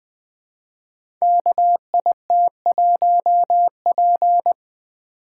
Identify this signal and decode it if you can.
Morse code, heard as a keyed tone, about 20 wpm, 700 Hz. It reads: KIT1P
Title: Code Groups - 5 Letter:Numbers - 20wpm